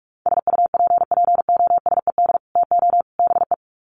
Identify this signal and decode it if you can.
Morse code, heard as a keyed tone, about 44 wpm, 700 Hz. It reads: SUPPOSED TO BE
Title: Sets of 3 Words 45wpm